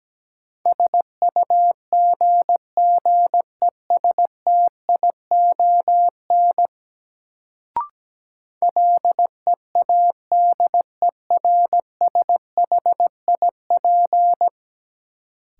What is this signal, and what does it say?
Morse code, heard as a keyed tone, about 17 wpm, 700 Hz. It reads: SUGGESTION LEADERSHIP
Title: Sentences from Top 1600 Words - New Word Drill Alt - 17wpm